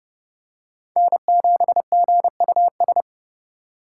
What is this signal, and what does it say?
Morse code, heard as a keyed tone, about 30 wpm, 700 Hz. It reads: N7GUS
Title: USA Call Signs - 4x Character Spacing - 30wpm